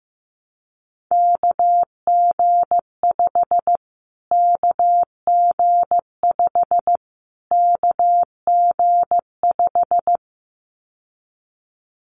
Morse code, heard as a keyed tone, about 15 wpm, 700 Hz. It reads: KG5 KG5 KG5
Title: Callsign Prefixes for Top 25 DXCC - Repeated 3x - 15wpm